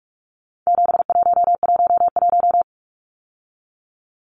Morse code, heard as a keyed tone, about 44 wpm, 700 Hz. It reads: 7111
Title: Numbers - 4-Digit 45wpm